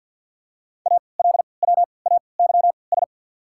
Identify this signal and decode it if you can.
Morse code, heard as a keyed tone, about 53 wpm, spaced as for 29 wpm, 700 Hz. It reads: ALWAYS
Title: 3x Character Spacing - Top 500 Words - 50wpm